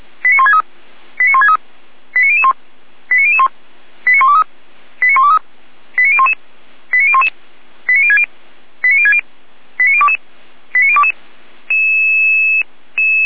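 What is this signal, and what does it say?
Five-tone selective call (ZVEI), heard as quick runs of five tones, tones of about 70 ms. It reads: 89163 89163 89001 89001 89123 89123 89011 89011 89060 89060 89022 89022
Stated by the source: Five-tone ZVEI selective call, address 89119, received on an SDR.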